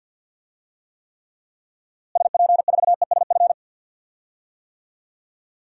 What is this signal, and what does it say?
Morse code, heard as a keyed tone, about 51 wpm, 700 Hz. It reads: SP4ESF
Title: DXCC-100 Call Signs - 6 Characters - Repeated 2x - 50wpm